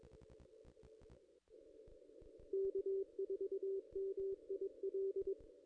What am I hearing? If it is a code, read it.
Morse code, heard as a keyed tone, about 22 wpm, 390 Hz. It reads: K4MIL
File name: contests / ss-cw-k5tr-op-n5rz -1700